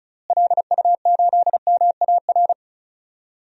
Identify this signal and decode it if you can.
Morse code, heard as a keyed tone, about 36 wpm, 700 Hz. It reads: LU8MAR
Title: DXCC-100 Call Signs - 6 Characters - Repeated 2x - 35wpm